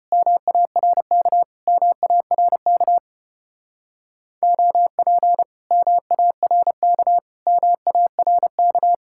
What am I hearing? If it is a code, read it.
Morse code, heard as a keyed tone, about 33 wpm, 700 Hz. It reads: MARK MARK OP MARK MARK